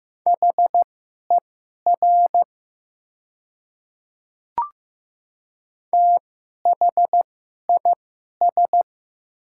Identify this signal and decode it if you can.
Morse code, heard as a keyed tone, about 15 wpm, spaced as for 11 wpm, 700 Hz. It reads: HER THIS
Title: 2x Character Spacing - Top 100 Words - 15wpm